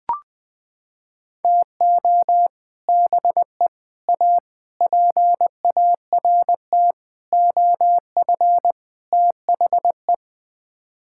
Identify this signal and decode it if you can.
Morse code, heard as a keyed tone, about 20 wpm, 700 Hz. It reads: TO BE A PART OF THE
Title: Sets of 6 Words - 20wpm